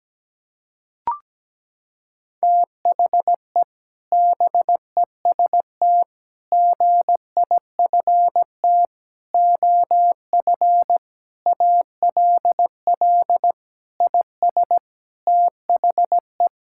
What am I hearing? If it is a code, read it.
Morse code, heard as a keyed tone, about 17 wpm, 700 Hz. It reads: THE BEST GIFT OF ALL IS THE
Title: Sentences from Top 1000 Words - Review Part 6 - 17wpm